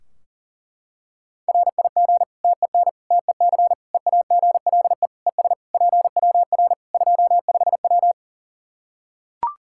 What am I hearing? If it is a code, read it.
Morse code, heard as a keyed tone, about 40 wpm, 700 Hz. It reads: RIG TEN TEC EAGLE ES PWR 25W
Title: K1USN SST QSO Practice 40wpm